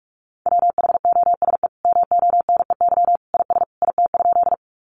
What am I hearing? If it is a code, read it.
Morse code, heard as a keyed tone, about 44 wpm, 700 Hz. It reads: WHOSE MONEY IS IT?
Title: Sentences from Top 1200 Words 45wpm